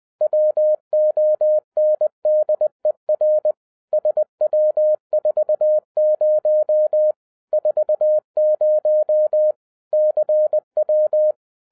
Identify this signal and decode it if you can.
Morse code, heard as a keyed tone, about 20 wpm, 610 Hz. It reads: WONDER SW40 40 CW
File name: 2011-07-mp3 / n23 QRP OTAKU 20wpm